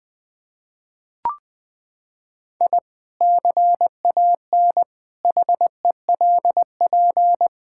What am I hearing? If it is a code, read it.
Morse code, heard as a keyed tone, about 20 wpm, 700 Hz. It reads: I CAN HELP
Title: Sentences with 2-Word Phrases - 20wpm